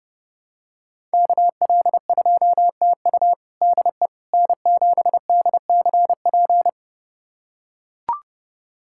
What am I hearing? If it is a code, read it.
Morse code, heard as a keyed tone, about 30 wpm, 700 Hz. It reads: KL2TU DE N7DCP
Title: USA Callsign DE Callsign - 5 Characters - 30wpm